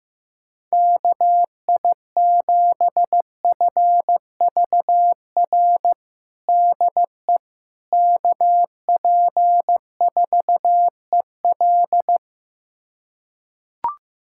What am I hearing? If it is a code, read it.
Morse code, heard as a keyed tone, about 16 wpm, 700 Hz. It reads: KI7FVR DE KP4EL
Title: USA Callsign DE Callsign 15wpm